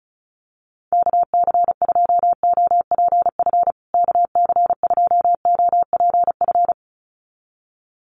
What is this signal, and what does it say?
Morse code, heard as a keyed tone, about 34 wpm, 700 Hz. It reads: KC2OPF KC2OPF
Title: USA Call Signs Repeated 2x 35wpm